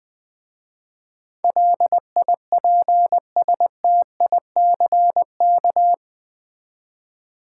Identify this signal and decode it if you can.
Morse code, heard as a keyed tone, about 20 wpm, 700 Hz. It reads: LIPSTICK
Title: Easy Words with Hints - 20wpm